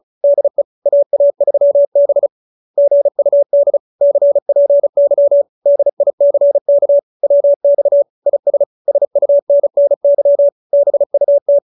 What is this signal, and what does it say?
Morse code, heard as a keyed tone, about 34 wpm, 570 Hz. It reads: DE AA3B GUD CPY DICK WX IS SUNNY BUT
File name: QSO302 35